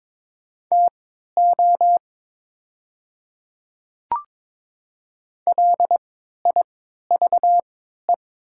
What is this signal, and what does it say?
Morse code, heard as a keyed tone, about 22 wpm, spaced as for 12 wpm, 700 Hz. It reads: TO LIVE
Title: 3x Character Spacing - Top 200 Words - 22wpm